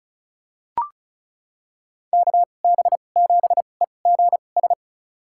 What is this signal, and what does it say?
Morse code, heard as a keyed tone, about 34 wpm, spaced as for 25 wpm, 690 Hz. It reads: KB7EGS
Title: USA Call Signs - 6 Characters - 2x Character Spacing - 35wpm